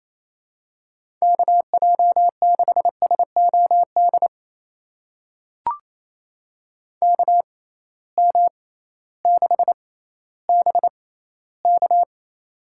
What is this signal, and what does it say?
Morse code, heard as a keyed tone, about 28 wpm, spaced as for 15 wpm, 700 Hz. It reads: KJ6SOD KM6BK
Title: USA Call Signs - 6x Character Spacing - 28wpm